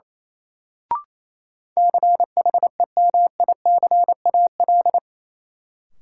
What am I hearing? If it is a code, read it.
Morse code, heard as a keyed tone, about 28 wpm, 700 Hz. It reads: CHEMICAL
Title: Sentences from Top 1200 Words - New Word Drill - 28wpm